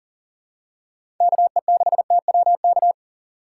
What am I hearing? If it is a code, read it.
Morse code, heard as a keyed tone, about 39 wpm, 700 Hz. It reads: KE6TWK
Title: USA Call Signs - 6 Characters - 6x Character Spacing - 40wpm